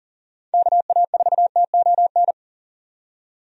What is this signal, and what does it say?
Morse code, heard as a keyed tone, about 37 wpm, 700 Hz. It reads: KA4TON
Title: USA Call Signs - 6 Characters - 40wpm